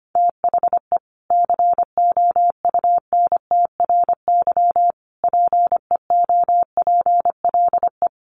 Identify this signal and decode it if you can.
Morse code, heard as a keyed tone, about 25 wpm, 700 Hz. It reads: THE COUNTRY PEOPLE
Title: Sentences from Top 1100 Words 25wpm